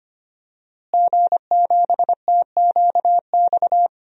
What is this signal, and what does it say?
Morse code, heard as a keyed tone, about 25 wpm, 700 Hz. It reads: G7TQX
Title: DXCC-100 Call Signs - 5 Characters - Repeated 3x - 25wpm